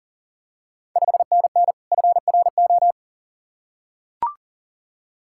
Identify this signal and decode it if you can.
Morse code, heard as a keyed tone, about 40 wpm, 700 Hz. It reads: 5NN FRO
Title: Ontario QSO Party Exchange 40wpm